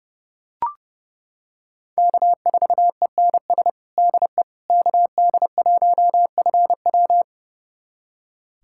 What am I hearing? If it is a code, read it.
Morse code, heard as a keyed tone, about 31 wpm, 700 Hz. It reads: K4ENS DE KD1FW
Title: USA Callsign DE Callsign 30wpm